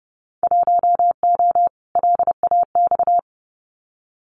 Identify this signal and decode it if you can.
Morse code, heard as a keyed tone, about 30 wpm, 700 Hz. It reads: 1O LAX
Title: Winter Field Day Exchanges 30wpm